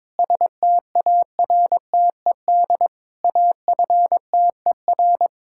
Morse code, heard as a keyed tone, about 22 wpm, 700 Hz. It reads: STARTED AFTER
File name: practice / 22wpm / Sentences with 2-Word Phrases - 22wpm